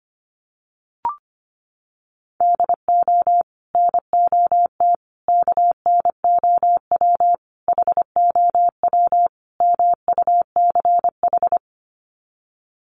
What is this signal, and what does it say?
Morse code, heard as a keyed tone, about 25 wpm, 700 Hz. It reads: DO NOT KNOW HOW MUCH
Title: Sets of 5 Words Encore 25wpm